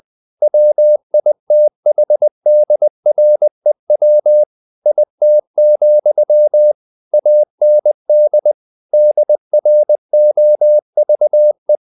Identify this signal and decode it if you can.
Morse code, heard as a keyed tone, about 20 wpm, 600 Hz. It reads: WITHDREW IT, AND DROVE
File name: the old man and the sea part 2 20wpm